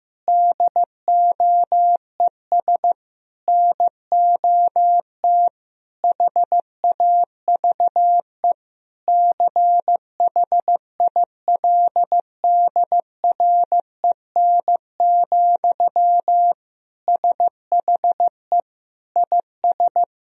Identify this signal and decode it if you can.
Morse code, heard as a keyed tone, about 15 wpm, 700 Hz. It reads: DOES NOT HAVE CHILDREN, SHE IS
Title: Sentences from Top 1000 Words - Review Part 3 - 15wpm